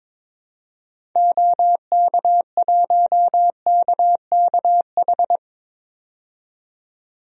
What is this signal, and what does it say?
Morse code, heard as a keyed tone, about 22 wpm, 700 Hz. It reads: OK1KKH
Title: DXCC-100 Call Signs - 6 Characters - 22wpm